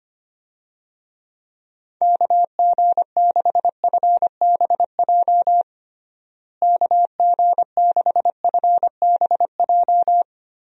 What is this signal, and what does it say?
Morse code, heard as a keyed tone, about 25 wpm, 700 Hz. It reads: KG6FBJ KG6FBJ
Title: USA Call Signs - 6 Characters - Repeated 2x - 25wpm